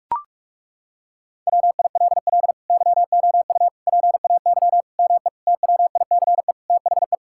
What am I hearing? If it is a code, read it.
Morse code, heard as a keyed tone, about 41 wpm, 700 Hz. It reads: WILL YOU PAY ME TWICE THE
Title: Sentences from Top 1600 Words 45wpm